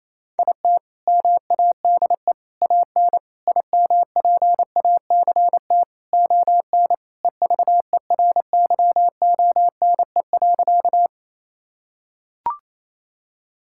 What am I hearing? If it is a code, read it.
Morse code, heard as a keyed tone, about 28 wpm, 700 Hz. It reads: IT MADE AN IMPACT ON EVERYONE.
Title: Sentences from Top 1000 Words - Review Part 6 - 28wpm